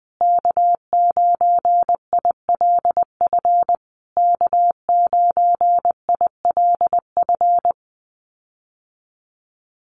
Morse code, heard as a keyed tone, about 20 wpm, 700 Hz. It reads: K9ILF K9ILF
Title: USA Call Signs Repeated 3x 20wpm